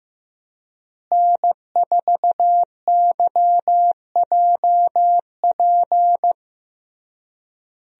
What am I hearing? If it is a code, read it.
Morse code, heard as a keyed tone, about 15 wpm, 700 Hz. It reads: N4YJP
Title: USA Call Signs - 5 Characters - Repeated 3x - 15wpm